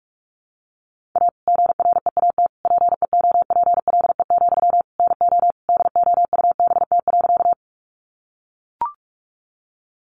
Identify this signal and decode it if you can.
Morse code, heard as a keyed tone, about 44 wpm, 700 Hz. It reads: A GREAT PEOPLE, NO DOUBT.
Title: Sentences from Top 1500 Words 45wpm